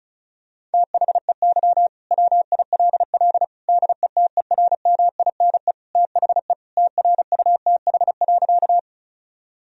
Morse code, heard as a keyed tone, about 36 wpm, 700 Hz. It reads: THEY WILL DETERMINE THE TRUTH.
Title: Sentences from Top 1000 Words - Review Part 4 - 35wpm